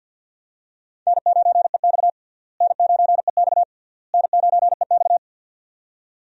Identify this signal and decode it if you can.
Morse code, heard as a keyed tone, about 52 wpm, 700 Hz. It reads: N9EX N9EX N9EX
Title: USA Call Signs - 4 Characters - Repeated 3x - 50wpm